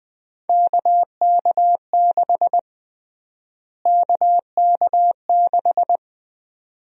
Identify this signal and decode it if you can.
Morse code, heard as a keyed tone, about 20 wpm, 700 Hz. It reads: KK6 KK6
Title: USA Call Sign Prefixes - Repeated 2x - 20wpm